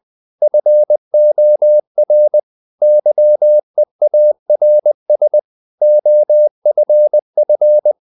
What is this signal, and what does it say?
Morse code, heard as a keyed tone, about 20 wpm, 600 Hz. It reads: FOR YEARS OFF
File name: the old man and the sea / the old man and the sea part 1 20wpm